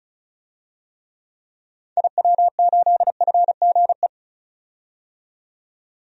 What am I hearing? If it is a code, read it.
Morse code, heard as a keyed tone, about 36 wpm, 700 Hz. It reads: IW8FGE
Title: DXCC-100 Call Signs - 6 Characters - 35wpm